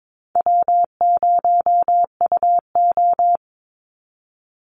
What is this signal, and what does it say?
Morse code, heard as a keyed tone, about 22 wpm, 700 Hz. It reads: W0UO
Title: Easy Call Signs Repeated 2x - 22wpm